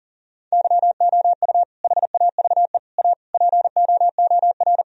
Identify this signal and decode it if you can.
Morse code, heard as a keyed tone, about 40 wpm, 700 Hz. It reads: YOU HAVE A POOR
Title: Sentences from Top 1000 Words - Review Part 3 - 40wpm